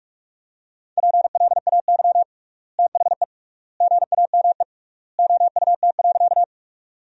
Morse code, heard as a keyed tone, about 46 wpm, 700 Hz. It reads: PLAY THE GAME OUT.
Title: 3x Word-Spacing - Sentences from Top 500 Words - 45wpm